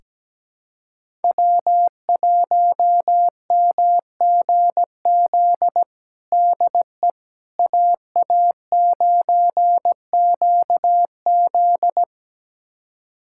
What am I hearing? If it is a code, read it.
Morse code, heard as a keyed tone, about 17 wpm, 700 Hz. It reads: W1MGZ DE AA9QZ
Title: USA Callsign DE Callsign - 5 Characters - 17wpm